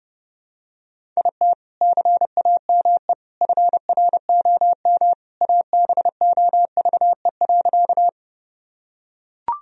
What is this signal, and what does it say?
Morse code, heard as a keyed tone, about 32 wpm, 700 Hz. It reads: IT CAME FROM ABOVE.
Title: Sentences from Top 1000 Words - Review Part 2 - 30wpm